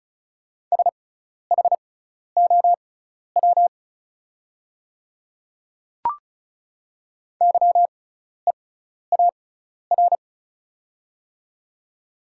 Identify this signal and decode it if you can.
Morse code, heard as a keyed tone, about 34 wpm, spaced as for 12 wpm, 690 Hz. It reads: SHOW YEAR
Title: 6x Character Spacing - Top 500 Words - 35wpm